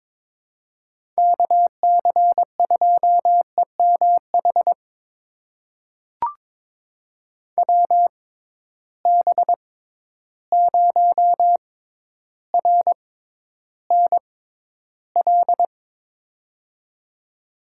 Morse code, heard as a keyed tone, about 23 wpm, spaced as for 11 wpm, 700 Hz. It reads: KC2EMH WB0RNL
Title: USA Call Signs - 6 Characters - 6x Character Spacing - 22wpm